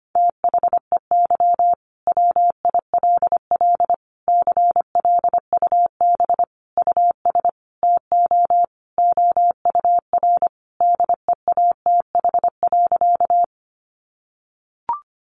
Morse code, heard as a keyed tone, about 25 wpm, 700 Hz. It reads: THEY WILL CLUB US TO OUR DEATH.
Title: Sentences from Top 1200 Words 25wpm